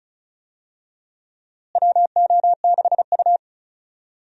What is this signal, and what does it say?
Morse code, heard as a keyed tone, about 36 wpm, 700 Hz. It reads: WO6U
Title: USA Call Signs - 4 Characters - Repeated 3x - 35wpm